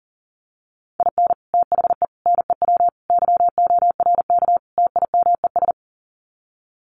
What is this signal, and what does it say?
Morse code, heard as a keyed tone, about 40 wpm, 700 Hz. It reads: IN THE NEW YORK TIMES
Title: Sets of 5 Words 40wpm